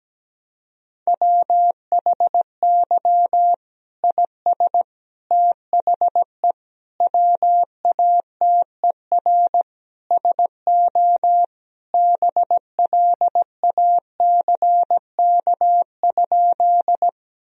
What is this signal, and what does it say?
Morse code, heard as a keyed tone, about 17 wpm, 700 Hz. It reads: WHY IS THE WATER SO BLACK?
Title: Sentences from Top 1000 Words - Review Part 2 - 17wpm